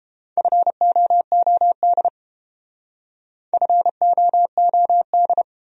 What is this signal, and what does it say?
Morse code, heard as a keyed tone, about 28 wpm, 700 Hz. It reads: FOOD FOOD